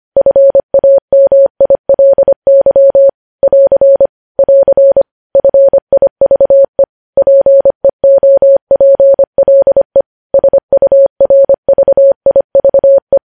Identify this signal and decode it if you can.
Morse code, heard as a keyed tone, about 25 wpm, 550 Hz. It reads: FAMILY <AR> <AR> FIVE PEOPLE SURVIVE
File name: podcast / News headlines-1x-25x25